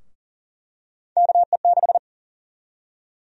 Morse code, heard as a keyed tone, about 40 wpm, 700 Hz. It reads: KE6
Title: USA Call Sign Prefixes - 40wpm